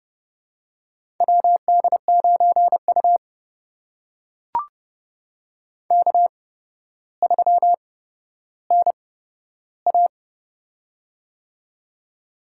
Morse code, heard as a keyed tone, about 30 wpm, spaced as for 13 wpm, 700 Hz. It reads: WD9U K3NA
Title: USA Call Signs - 4 Characters - 8x Character Spacing - 30wpm